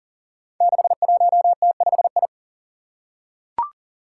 Morse code, heard as a keyed tone, about 40 wpm, 700 Hz. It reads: 61T5I
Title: Five Challenging Letter-Number Combinations 40wpm